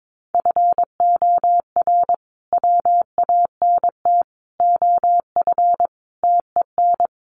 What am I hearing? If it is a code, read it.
Morse code, heard as a keyed tone, about 22 wpm, 700 Hz. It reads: FOR WANT OF TEN
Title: A Sound Of Thunder 22wpm